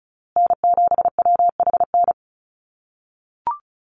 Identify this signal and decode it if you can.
Morse code, heard as a keyed tone, about 34 wpm, 700 Hz. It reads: N7WHN
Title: USA Call Signs Repeated 2x 35wpm